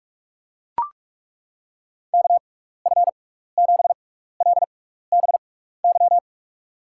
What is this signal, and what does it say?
Morse code, heard as a keyed tone, about 41 wpm, spaced as for 15 wpm, 700 Hz. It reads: KF7LBY
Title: USA Call Signs - 6 Characters - 6x Character Spacing - 45wpm